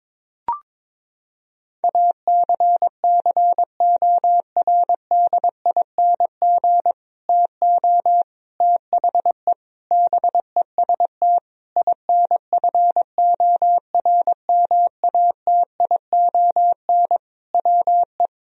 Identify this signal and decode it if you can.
Morse code, heard as a keyed tone, about 22 wpm, 700 Hz. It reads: ACCORDING TO THE BEST INFORMATION WE
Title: Sentences from Top 1000 Words - Review Part 4 - 22wpm